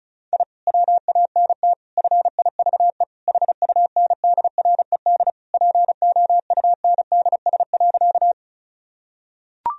Morse code, heard as a keyed tone, about 34 wpm, 700 Hz. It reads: I WANT FIVE HUNDRED POUNDS.
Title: Sentences from Top 1000 Words - Review Part 2 - 35wpm